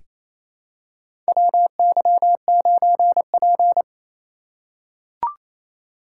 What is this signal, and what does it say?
Morse code, heard as a keyed tone, about 28 wpm, 700 Hz. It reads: WY9P
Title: USA Call Signs - 4 Characters - 2x Character Spacing - 28wpm